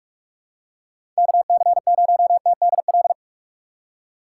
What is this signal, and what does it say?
Morse code, heard as a keyed tone, about 41 wpm, 700 Hz. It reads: KC0TDL
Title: USA Call Signs - 6 Characters - 3x Character Spacing - 45wpm